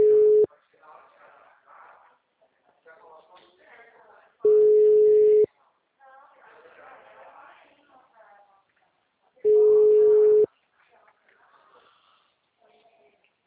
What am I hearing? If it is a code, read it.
Ringback tone (Europe, 425 Hz), with speech.